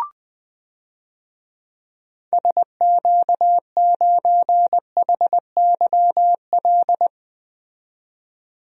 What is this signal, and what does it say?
Morse code, heard as a keyed tone, about 20 wpm, 700 Hz. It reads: SQ9HYL
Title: DXCC-100 Call Signs - 6 Characters - 20wpm